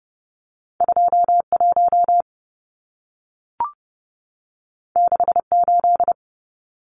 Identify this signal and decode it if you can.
Morse code, heard as a keyed tone, about 30 wpm, 700 Hz. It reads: 21 68
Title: Numbers - 2-Digit 30wpm